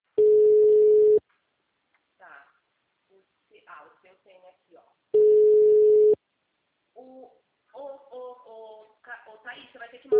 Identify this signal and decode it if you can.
Ringback tone (Europe, 425 Hz), with speech.